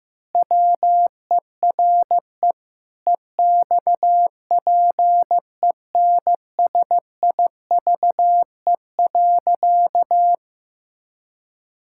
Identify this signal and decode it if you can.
Morse code, heard as a keyed tone, about 15 wpm, 700 Hz. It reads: WERE EXPENSIVE.
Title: Pangram Challenge v2 - 15wpm